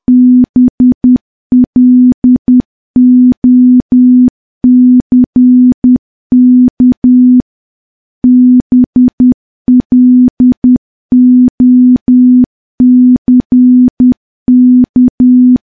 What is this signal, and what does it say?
Morse code, heard as a keyed tone, about 10 wpm, 250 Hz. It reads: BLOCK BLOCK
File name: morse / morse-training-10WPM